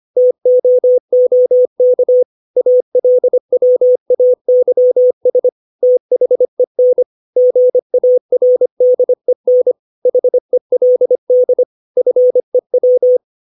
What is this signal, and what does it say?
Morse code, heard as a keyed tone, about 25 wpm, 500 Hz. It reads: TOOK ALWAYS THEN GARDEN HELD FEW
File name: 632 words / 632 Words 25WPM